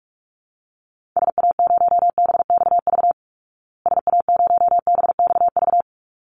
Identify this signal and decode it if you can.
Morse code, heard as a keyed tone, about 44 wpm, 700 Hz. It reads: SA0BXV SA0BXV
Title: Call Signs Repeated 3x - 45wpm